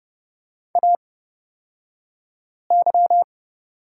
Morse code, heard as a keyed tone, about 30 wpm, spaced as for 5 wpm, 700 Hz. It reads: AY
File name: Lesson 37d - Letter x - Two Characters 30wpm